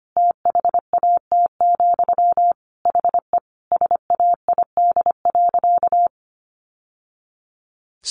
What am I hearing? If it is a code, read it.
Morse code, heard as a keyed tone, about 25 wpm, 700 Hz. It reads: THAT, HE SAID.
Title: Sentences from Top 1000 Words 25wpm